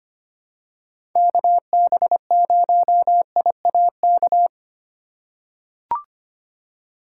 Morse code, heard as a keyed tone, about 25 wpm, 700 Hz. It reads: KB0IAK
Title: USA Call Signs - 6 Characters - 3x Character Spacing - 25wpm